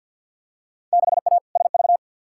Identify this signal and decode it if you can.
Morse code, heard as a keyed tone, about 51 wpm, 700 Hz. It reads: 6A SV
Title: Field Day Exchanges Encore - 50wpm